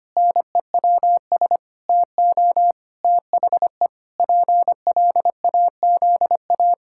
Morse code, heard as a keyed tone, about 25 wpm, 700 Hz. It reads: NEWS TO THE PLAZA
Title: Pangram Challenge v2 - 25wpm